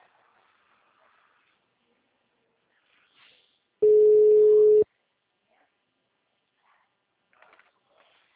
Ringback tone (Europe, 425 Hz).
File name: out- -2019- - - .21897